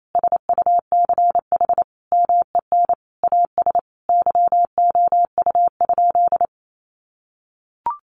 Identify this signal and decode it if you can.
Morse code, heard as a keyed tone, about 28 wpm, 700 Hz. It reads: SUCH MEN AS YOU?
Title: Sentences from Top 1000 Words 28wpm